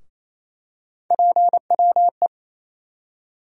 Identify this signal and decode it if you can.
Morse code, heard as a keyed tone, about 28 wpm, 700 Hz. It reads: PWE
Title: USA Call Sign Postfixes - 3 Letters 28wpm